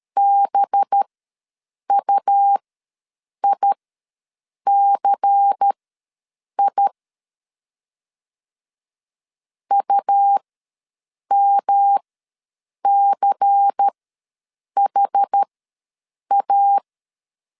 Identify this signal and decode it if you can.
Morse code, heard as a keyed tone, about 13 wpm, spaced as for 7 wpm, 800 Hz. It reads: BUICI UMCHA
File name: LEZIONI / 5 LBVC / 5 LBVC ws 12wpm 3159